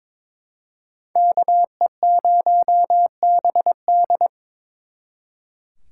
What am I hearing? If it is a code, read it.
Morse code, heard as a keyed tone, about 26 wpm, 700 Hz. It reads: KE0BD
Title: Speed Racing - USA 2x2 Call Signs 22wpm